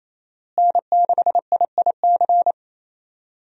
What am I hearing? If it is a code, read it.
Morse code, heard as a keyed tone, about 28 wpm, 700 Hz. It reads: N6IIC
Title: USA Call Signs - 5 Characters - Repeated 3x - 28wpm